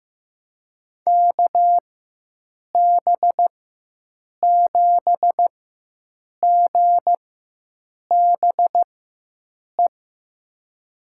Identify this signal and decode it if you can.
Morse code, heard as a keyed tone, about 15 wpm, spaced as for 7 wpm, 700 Hz. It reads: KB7GBE
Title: USA Call Signs - 6 Characters - 4x Character Spacing - 15wpm